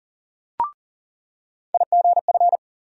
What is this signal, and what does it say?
Morse code, heard as a keyed tone, about 40 wpm, 700 Hz. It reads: IGF
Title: USA Call Sign Postfixes - 3 Letters 40wpm